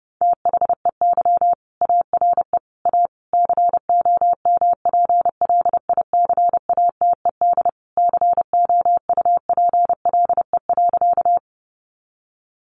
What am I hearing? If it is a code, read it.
Morse code, heard as a keyed tone, about 30 wpm, 700 Hz. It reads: THEY ARE A COMPLICATED COUPLE.
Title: Sentences from Top 1200 Words 30wpm